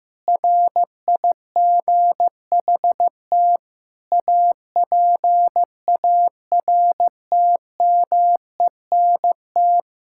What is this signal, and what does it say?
Morse code, heard as a keyed tone, about 15 wpm, 700 Hz. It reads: RIGHT APARTMENT
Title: Sentences from Top 1000 Words - Review Part 6 - 15wpm